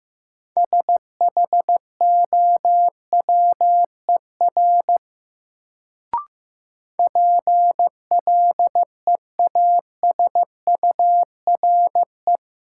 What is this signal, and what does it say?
Morse code, heard as a keyed tone, about 15 wpm, 700 Hz. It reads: SHOWER PLEASURE
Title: Sentences from Top 1600 Words - New Word Drill - 15wpm